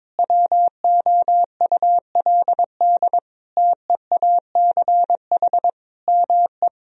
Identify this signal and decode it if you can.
Morse code, heard as a keyed tone, about 21 wpm, 700 Hz. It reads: WOULD TEACH ME
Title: Sentences from Top 1000 Words - Review Part 2 - 22wpm